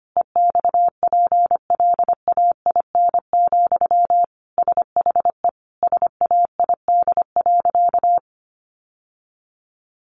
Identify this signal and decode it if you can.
Morse code, heard as a keyed tone, about 25 wpm, 700 Hz. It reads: EXPLAIN, SHE SAID.
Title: Sentences from Top 1100 Words 25wpm